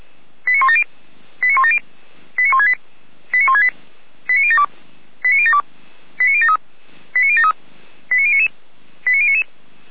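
Five-tone selective call (ZVEI), heard as quick runs of five tones, tones of about 70 ms. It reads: 89170 89170 89168 89168 89072 89072 89073 89073 89099 89099
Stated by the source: ZVEI five-tone selective call received on an SDR.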